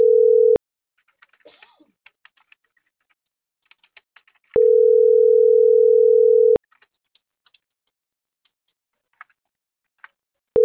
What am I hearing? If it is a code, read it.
Ringback tone (North America), 440 + 480 Hz.
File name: CENTRAL MULTIMODAL / out francisco.cobos - .38755